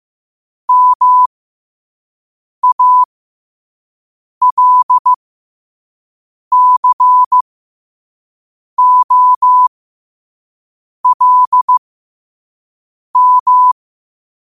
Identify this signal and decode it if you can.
Morse code, heard as a keyed tone, about 15 wpm, spaced as for 5 wpm, 1000 Hz. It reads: MALCOLM
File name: qotdfiles / 5wpm